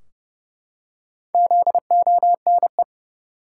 Morse code, heard as a keyed tone, about 32 wpm, 700 Hz. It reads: ZONE
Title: Four Letter Words - 30wpm